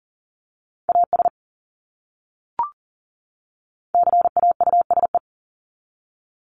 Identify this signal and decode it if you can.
Morse code, heard as a keyed tone, about 40 wpm, 700 Hz. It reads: AS CAUSE
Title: Top 300 Words 40wpm